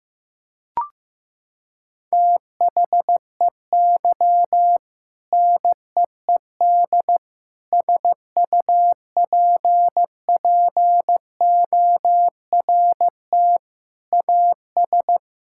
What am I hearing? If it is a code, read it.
Morse code, heard as a keyed tone, about 15 wpm, 700 Hz. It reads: THEY NEED SUPPORT AS
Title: Sentences from Top 1000 Words - Review Part 5 - 15wpm